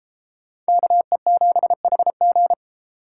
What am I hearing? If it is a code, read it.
Morse code, heard as a keyed tone, about 33 wpm, 700 Hz. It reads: KE7HG